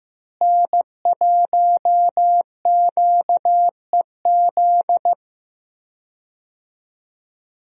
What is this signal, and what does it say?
Morse code, heard as a keyed tone, about 16 wpm, 700 Hz. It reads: N1QEZ
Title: USA Call Signs - 5 Characters - Repeated 3x - 15wpm